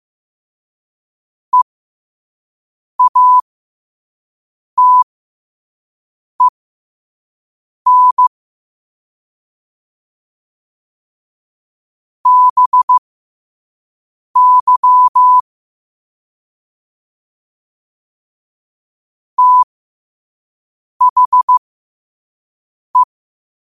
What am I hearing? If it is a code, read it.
Morse code, heard as a keyed tone, about 15 wpm, spaced as for 5 wpm, 1000 Hz. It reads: EATEN BY THE
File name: qotdfiles / 5wpm